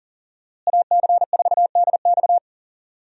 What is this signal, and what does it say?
Morse code, heard as a keyed tone, about 40 wpm, 700 Hz. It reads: AC4DX
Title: USA Call Signs - 5 Characters - Repeated 3x - 40wpm